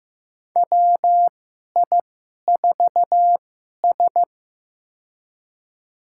Morse code, heard as a keyed tone, about 15 wpm, spaced as for 11 wpm, 700 Hz. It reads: WI4S
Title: USA Call Signs - 4 Characters - 2x Character Spacing - 15wpm